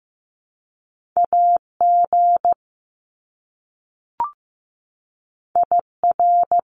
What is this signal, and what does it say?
Morse code, heard as a keyed tone, about 15 wpm, 700 Hz. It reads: AG IR
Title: Top Two Letter Combinations in Words 15wpm